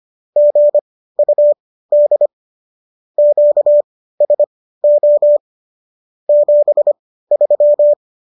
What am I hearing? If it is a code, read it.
Morse code, heard as a keyed tone, about 25 wpm, spaced as for 15 wpm, 590 Hz. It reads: GUD QSO 73